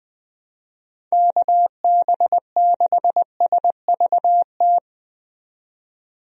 Morse code, heard as a keyed tone, about 20 wpm, 700 Hz. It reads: KB6SVT
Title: USA Call Signs - 6 Characters - 20wpm